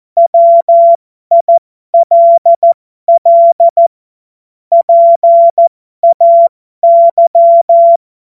Morse code, heard as a keyed tone, about 14 wpm, spaced as for 12 wpm, 680 Hz. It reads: WILL PAY